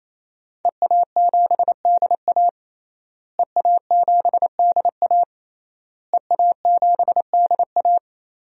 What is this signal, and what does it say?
Morse code, heard as a keyed tone, about 28 wpm, 700 Hz. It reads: EA7DA EA7DA EA7DA
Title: DXCC-100 Call Signs - 5 Characters - Repeated 3x - 28wpm